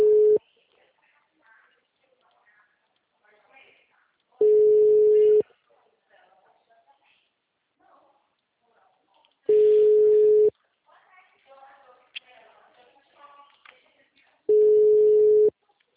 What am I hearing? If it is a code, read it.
Ringback tone (Europe, 425 Hz), with speech.